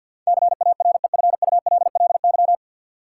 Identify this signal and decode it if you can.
Morse code, heard as a keyed tone, about 53 wpm, 700 Hz. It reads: CAREFULLY
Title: Sentences from Top 1300 Words - New Word Drill Alt - 50wpm